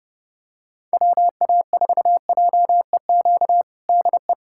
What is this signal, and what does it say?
Morse code, heard as a keyed tone, about 30 wpm, 700 Hz. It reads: WA4JEQ DE
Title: USA Callsign DE Callsign 30wpm